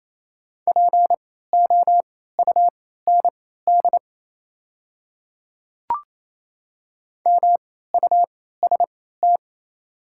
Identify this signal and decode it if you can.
Morse code, heard as a keyed tone, about 28 wpm, spaced as for 16 wpm, 690 Hz. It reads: POUND MUST
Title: 3x Character Spacing - Top 500 Words - 28wpm